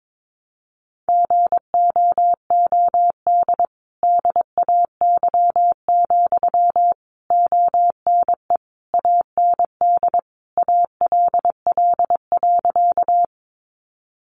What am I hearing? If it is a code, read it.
Morse code, heard as a keyed tone, about 22 wpm, 700 Hz. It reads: GOOD DAY, ONE AND ALL.
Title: Sentences from Top 200 Words 22wpm